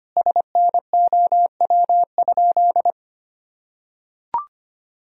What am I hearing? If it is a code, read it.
Morse code, heard as a keyed tone, about 25 wpm, 700 Hz. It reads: SNOW?
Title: Sentences from Top 1000 Words - Review Part 3 - 25wpm